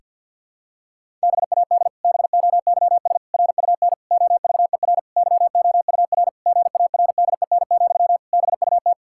Morse code, heard as a keyed tone, about 50 wpm, 700 Hz. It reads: BAD BOYS RUN OVER YOUR GARDEN, BUT
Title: Pangram Challenge v2 - 50wpm